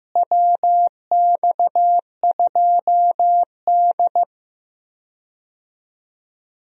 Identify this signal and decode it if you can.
Morse code, heard as a keyed tone, about 15 wpm, 700 Hz. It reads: WX2D
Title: USA Call Signs - 4 Characters - Repeated 2x - 15wpm